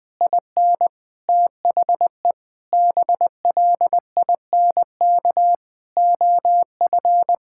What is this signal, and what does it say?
Morse code, heard as a keyed tone, about 20 wpm, 700 Hz. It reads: IN THE BLINK OF
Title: Sentences with 6-Word Phrases - 20wpm